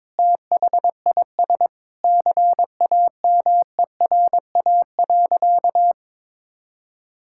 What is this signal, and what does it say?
Morse code, heard as a keyed tone, about 21 wpm, 700 Hz. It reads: THIS CAMERA.
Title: Sentences from Top 1000 Words - Review Part 5 - 22wpm